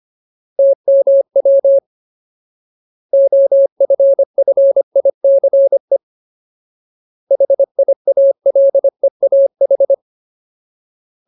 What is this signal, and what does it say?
Morse code, heard as a keyed tone, about 25 wpm, 550 Hz. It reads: TMW OFFICE HIALEAH
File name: Ultimate Word List / 25WPM Ultimate